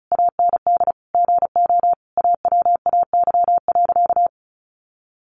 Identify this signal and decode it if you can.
Morse code, heard as a keyed tone, about 34 wpm, 700 Hz. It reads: AND GO AWAY.
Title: Sentences from Top 1100 Words 35wpm